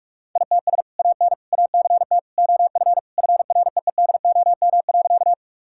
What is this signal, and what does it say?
Morse code, heard as a keyed tone, about 41 wpm, 700 Hz. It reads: ITS AN ACT OF FREEDOM.
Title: Sentences from Top 1000 Words - Review Part 5 - 45wpm